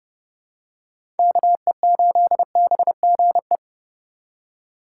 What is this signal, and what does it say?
Morse code, heard as a keyed tone, about 30 wpm, 700 Hz. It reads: KE8BGE
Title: USA Call Signs - 6 Characters - 3x Character Spacing - 30wpm